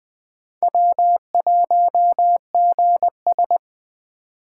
Morse code, heard as a keyed tone, about 24 wpm, 700 Hz. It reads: W1GS
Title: Speed Racing - USA 1x2 Call Signs 20wpm